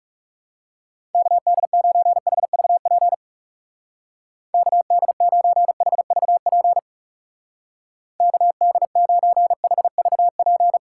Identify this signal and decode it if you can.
Morse code, heard as a keyed tone, about 38 wpm, 700 Hz. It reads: KD9HVP KD9HVP KD9HVP